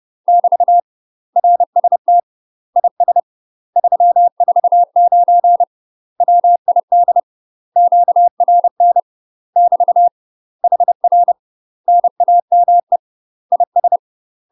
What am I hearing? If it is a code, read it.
Morse code, heard as a keyed tone, about 30 wpm, 700 Hz. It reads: <BT> RST IS 349 WID QRN <BT> HR NAME IS
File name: UK MASTER QSOs TRAINING / 30WPM / UK MASTER QSOs 0044